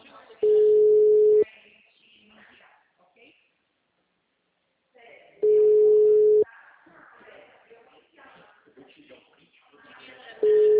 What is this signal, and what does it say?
Ringback tone (Europe, 425 Hz), with speech.